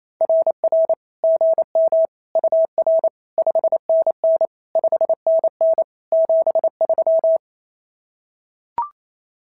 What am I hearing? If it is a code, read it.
Morse code, heard as a keyed tone, about 28 wpm, 650 Hz. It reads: RR GM UR 5NN 5NN 73
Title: SOTA QSO Practice 28wpm